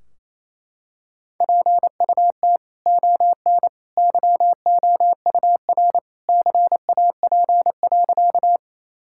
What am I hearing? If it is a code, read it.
Morse code, heard as a keyed tone, about 28 wpm, 700 Hz. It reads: PUT ON YOUR CAP.
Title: Sentences from Top 1600 Words 28wpm